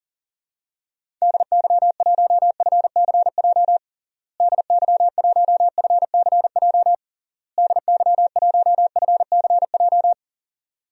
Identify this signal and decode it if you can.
Morse code, heard as a keyed tone, about 37 wpm, 700 Hz. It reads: DY1FCJ DY1FCJ DY1FCJ
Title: DXCC-100 Call Signs - 6 Characters - Repeated 3x - 40wpm